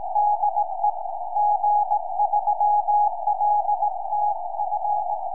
Morse code, heard as a keyed tone, about 19 wpm, 830 Hz. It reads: DE G3LTF
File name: tropo / g3ltf 70cm